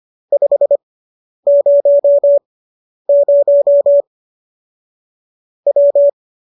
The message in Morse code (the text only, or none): 500 W